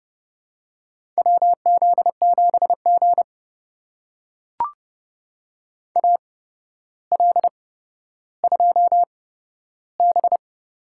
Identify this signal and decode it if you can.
Morse code, heard as a keyed tone, about 32 wpm, spaced as for 13 wpm, 700 Hz. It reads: WZ7G AL2B
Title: USA Call Signs - 4 Characters - 8x Character Spacing - 30wpm